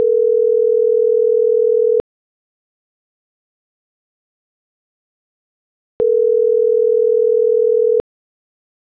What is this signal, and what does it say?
Ringback tone (North America), 440 + 480 Hz.